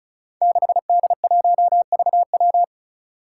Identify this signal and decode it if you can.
Morse code, heard as a keyed tone, about 34 wpm, 700 Hz. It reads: 6D1VW
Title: Five Challenging Letter-Number Combinations 35wpm